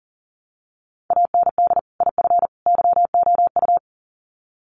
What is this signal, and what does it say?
Morse code, heard as a keyed tone, about 40 wpm, 700 Hz. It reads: AND IF YOU
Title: Sets of 3 Words 40wpm